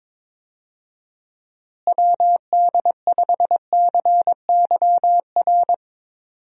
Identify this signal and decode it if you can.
Morse code, heard as a keyed tone, about 22 wpm, 700 Hz. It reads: WD5CYR
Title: USA Call Signs - 6 Characters - Repeated 3x - 22wpm